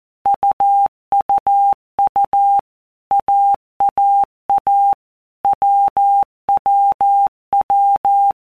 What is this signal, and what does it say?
Morse code, heard as a keyed tone, about 14 wpm, 800 Hz. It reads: UUU AAA WWW
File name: Basic Letter Groups By Blocks / Groups@13 / AWUNDG@13